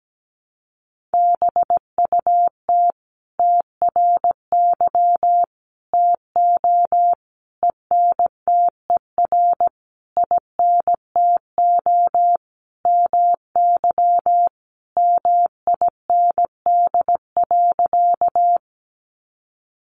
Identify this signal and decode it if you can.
Morse code, heard as a keyed tone, about 17 wpm, 700 Hz. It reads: BUT TRY TO ENTER INTO MY MIND.
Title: Sentences from Top 1300 Words 17wpm